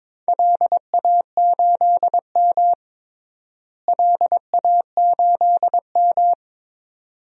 Morse code, heard as a keyed tone, about 22 wpm, 700 Hz. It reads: LA8M LA8M
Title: DXCC-100 Call Signs - 4 Characters - Repeated 2x - 22wpm